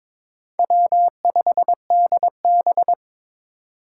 Morse code, heard as a keyed tone, about 22 wpm, 700 Hz. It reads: W5DB
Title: USA Call Signs - 4 Characters - 4x Character Spacing - 22wpm